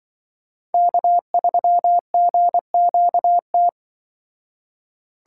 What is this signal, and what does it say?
Morse code, heard as a keyed tone, about 24 wpm, 700 Hz. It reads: K3GQT
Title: Speed Racing - USA 1x3 Call Signs 20wpm